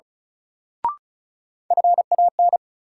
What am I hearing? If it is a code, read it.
Morse code, heard as a keyed tone, about 34 wpm, 700 Hz. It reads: FAN
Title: Sentences from Top 1000 Words - Review Part 4 - New Word Drill - 35wpm